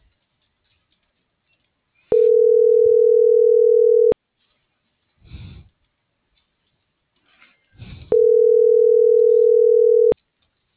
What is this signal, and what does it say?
Ringback tone (North America), 440 + 480 Hz.